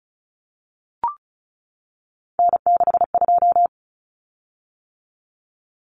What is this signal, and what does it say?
Morse code, heard as a keyed tone, about 34 wpm, 700 Hz. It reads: N62
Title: Common Cut Numbers - 3-Digit 35wpm